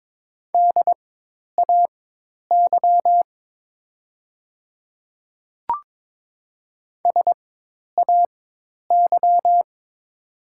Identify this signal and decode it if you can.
Morse code, heard as a keyed tone, about 22 wpm, spaced as for 10 wpm, 690 Hz. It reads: DAY SAY
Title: 4x Character Spacing - Top 200 Words - 22wpm